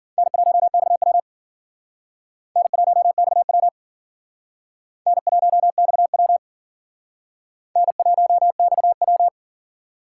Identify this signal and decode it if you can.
Morse code, heard as a keyed tone, about 53 wpm, 700 Hz. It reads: N1XW N1XW N1XW N1XW